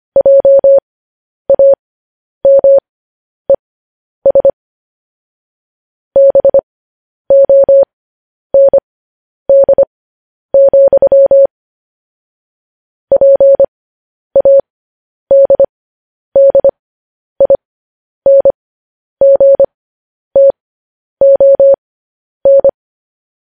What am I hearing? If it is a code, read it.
Morse code, heard as a keyed tone, about 25 wpm, spaced as for 10 wpm, 550 Hz. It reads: JAMES BOND, PADDINGTON
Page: news headlines in morse code at 10 wpm